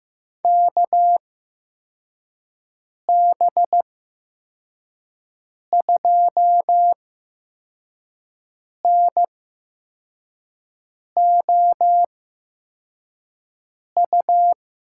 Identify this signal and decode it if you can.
Morse code, heard as a keyed tone, about 15 wpm, spaced as for 4 wpm, 700 Hz. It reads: KB2NOU
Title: USA Call Signs - 6 Characters - 8x Character Spacing - 15wpm